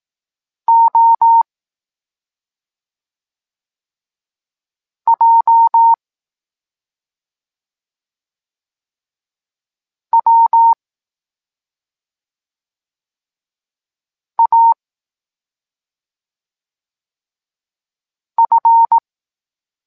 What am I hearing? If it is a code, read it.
Morse code, heard as a keyed tone, about 18 wpm, 920 Hz. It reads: O J W A F